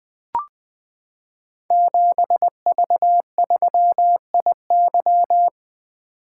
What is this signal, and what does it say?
Morse code, heard as a keyed tone, about 20 wpm, 700 Hz. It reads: 7V3IY
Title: Code Groups - 5 Letter:Numbers - 20wpm